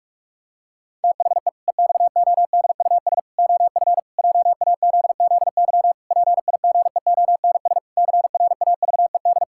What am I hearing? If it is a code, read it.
Morse code, heard as a keyed tone, about 41 wpm, 700 Hz. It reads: THE EXODUS OF JAZZY PIGEONS CRAVED
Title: Pangram Challenge v2 - 45wpm